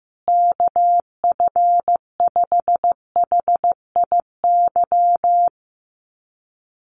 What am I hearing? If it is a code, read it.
Morse code, heard as a keyed tone, about 15 wpm, 700 Hz. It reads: KF5HIY
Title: USA Call Signs Repeated 3x 15wpm